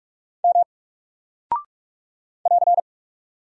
Morse code, heard as a keyed tone, about 44 wpm, 700 Hz. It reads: M <AR>
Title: Single Character (Letter:Number:Punctuation:Prosign) 45wpm